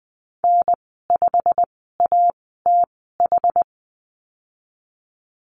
Morse code, heard as a keyed tone, about 20 wpm, spaced as for 15 wpm, 700 Hz. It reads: N5ATH
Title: 122 Lesson 20g - Number 3 - Call Signs 20wpm